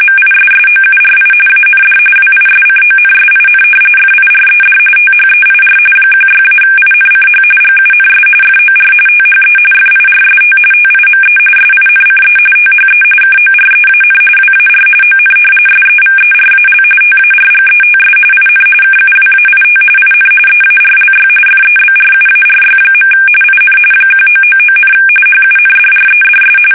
RTTY, about 45.45 baud. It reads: MISSOURI AND WISCONSIN FIRED THEIR 16-INCH GUNS FOR THE FIRST TIME SINCE THE KOREAN WAR ON SEVERAL TARGETS IN KUWAIT IN EARLY FEBRUARY. IN 1999, HUNDREDS
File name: fsk850 / usnavy5